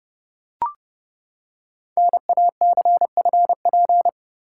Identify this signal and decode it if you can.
Morse code, heard as a keyed tone, about 30 wpm, 700 Hz. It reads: NACFP
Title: Code Groups - 5 Letter:Numbers - 30wpm